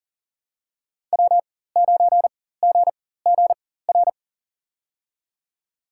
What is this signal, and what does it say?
Morse code, heard as a keyed tone, about 40 wpm, spaced as for 19 wpm, 700 Hz. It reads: W9GGR
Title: USA Call Signs - 5 Characters - 4x Character Spacing - 40wpm